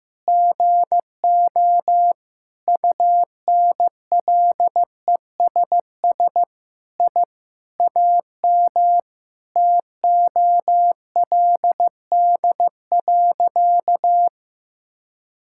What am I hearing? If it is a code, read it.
Morse code, heard as a keyed tone, about 15 wpm, 700 Hz. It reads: GO UNLESS I AM TOLD.
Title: Sentences from Top 1000 Words - Review Part 6 - 15wpm